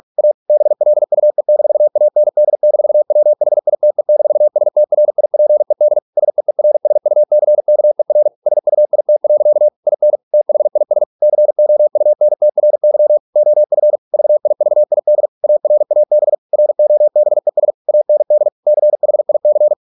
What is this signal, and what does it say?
Morse code, heard as a keyed tone, about 46 wpm, 600 Hz. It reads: A BLUE-AND-WHITE-STRIPED SEERSUCKER SUIT. IN THIS COUNTRY OF VIVID ARAB ROBES AND CHIC